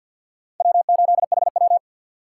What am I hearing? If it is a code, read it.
Morse code, heard as a keyed tone, about 51 wpm, 700 Hz. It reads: W8HW
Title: K1USN SST QSO Practice - 50wpm